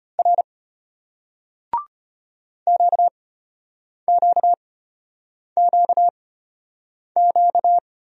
Morse code, heard as a keyed tone, about 34 wpm, spaced as for 8 wpm, 700 Hz. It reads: R QQQQ